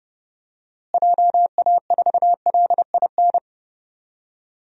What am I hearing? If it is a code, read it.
Morse code, heard as a keyed tone, about 32 wpm, 700 Hz. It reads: JA4LIN
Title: DXCC-100 Call Signs - 6 Characters - Repeated 2x - 30wpm